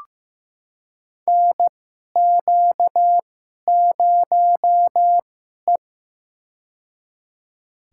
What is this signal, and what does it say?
Morse code, heard as a keyed tone, about 15 wpm, spaced as for 11 wpm, 700 Hz. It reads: NQ0E
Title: USA Call Signs - 4 Characters - 2x Character Spacing - 15wpm